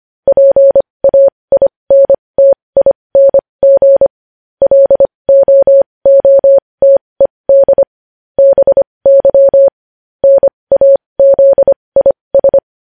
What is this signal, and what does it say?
Morse code, heard as a keyed tone, about 25 wpm, spaced as for 20 wpm, 550 Hz. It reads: PAINTING LOOTED BY NAZIS
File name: podcast / News headlines-1x-25x20